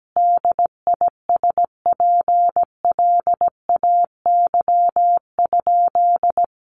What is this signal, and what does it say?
Morse code, heard as a keyed tone, about 17 wpm, 700 Hz. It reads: DISPLAY?
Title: Sentences from Top 1200 Words 17wpm